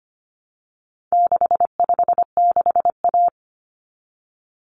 Morse code, heard as a keyed tone, about 25 wpm, 700 Hz. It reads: 656A
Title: Common Cut Numbers - 4-Digit 25wpm